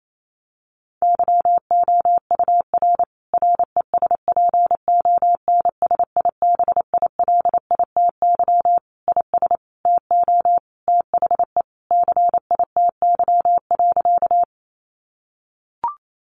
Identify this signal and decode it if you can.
Morse code, heard as a keyed tone, about 28 wpm, 700 Hz. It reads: YOUR RESPONSIBILITY IS TO THE CITY.
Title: Sentences from Top 1200 Words 28wpm